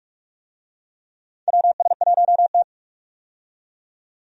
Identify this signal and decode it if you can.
Morse code, heard as a keyed tone, about 48 wpm, 700 Hz. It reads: WS1T
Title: USA Call Signs - 4 Characters - 45wpm